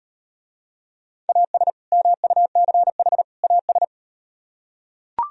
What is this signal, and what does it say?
Morse code, heard as a keyed tone, about 38 wpm, 700 Hz. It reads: AS MUCH AS